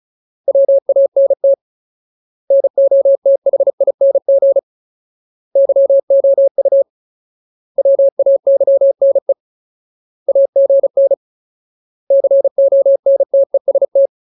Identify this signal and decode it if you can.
Morse code, heard as a keyed tone, about 34 wpm, 550 Hz. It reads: WANT NOTHING YOU WAYNE AGN CONTEST
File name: Ultimate Word List / 35WPM Ultimate